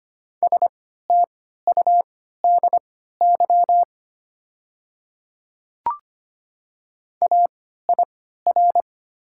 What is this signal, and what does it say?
Morse code, heard as a keyed tone, about 25 wpm, spaced as for 14 wpm, 700 Hz. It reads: STUDY AIR
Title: 3x Character Spacing - Top 300 Words - 25wpm